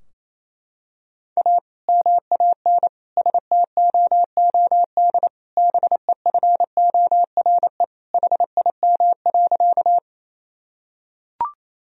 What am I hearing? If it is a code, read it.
Morse code, heard as a keyed tone, about 27 wpm, 700 Hz. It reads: A MAN STOOD BEFORE HIM.
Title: Sentences from Top 1000 Words - Review Part 2 - 28wpm